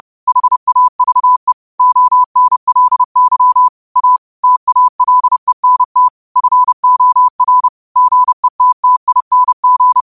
Morse code, heard as a keyed tone, about 30 wpm, 1000 Hz. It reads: HAVE ONLY A TALENT FOR GETTING
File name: qotdfiles / 30wpm